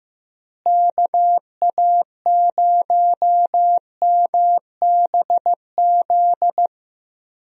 Morse code, heard as a keyed tone, about 15 wpm, 700 Hz. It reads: KA0MBZ
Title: USA Call Signs - 6 Characters - Repeated 2x - 15wpm